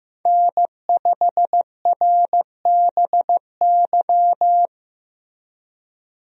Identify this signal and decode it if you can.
Morse code, heard as a keyed tone, about 15 wpm, 700 Hz. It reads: N5RBY
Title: USA Call Signs - 5 Characters - Repeated 3x - 15wpm